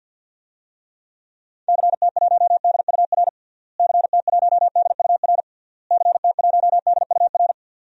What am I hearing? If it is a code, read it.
Morse code, heard as a keyed tone, about 52 wpm, 700 Hz. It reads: CT1DUR CT1DUR CT1DUR
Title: DXCC-100 Call Signs - 6 Characters - Repeated 3x - 50wpm